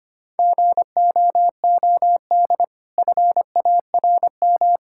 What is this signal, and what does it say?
Morse code, heard as a keyed tone, about 25 wpm, 700 Hz. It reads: GOOD FARM
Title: Sentences from Top 1000 Words - Review Part 5 - 25wpm